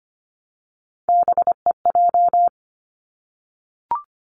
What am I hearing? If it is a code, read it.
Morse code, heard as a keyed tone, about 25 wpm, 700 Hz. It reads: BEJ
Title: Three Challenging Letter-Number Combinations 25wpm